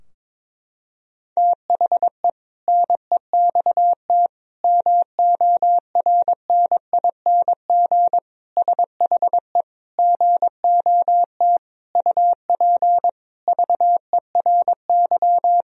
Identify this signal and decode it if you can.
Morse code, heard as a keyed tone, about 22 wpm, 700 Hz. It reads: THE NEXT MORNING SHE GOT UP VERY
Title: Sentences from Top 1000 Words - Review Part 2 - 22wpm